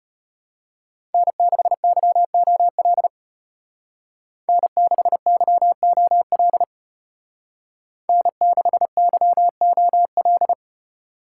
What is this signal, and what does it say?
Morse code, heard as a keyed tone, about 32 wpm, 700 Hz. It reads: N6YOL N6YOL N6YOL